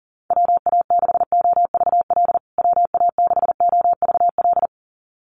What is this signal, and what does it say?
Morse code, heard as a keyed tone, about 40 wpm, 700 Hz. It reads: WA6OVL WA6OVL
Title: USA Call Signs Repeated 2x 40wpm